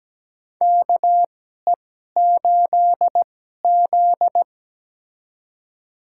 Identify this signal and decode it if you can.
Morse code, heard as a keyed tone, about 17 wpm, spaced as for 12 wpm, 700 Hz. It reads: KE8Z
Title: USA Call Signs - 2x Character Spacing - 17wpm